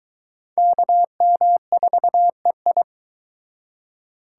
Morse code, heard as a keyed tone, about 20 wpm, 700 Hz. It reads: KM4EI